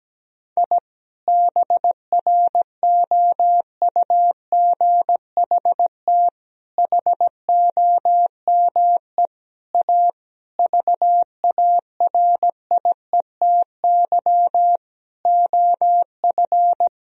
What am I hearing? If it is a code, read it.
Morse code, heard as a keyed tone, about 17 wpm, 700 Hz. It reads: I BROUGHT HOME A VARIETY OF
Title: Sentences from Top 1000 Words - Review Part 4 - 17wpm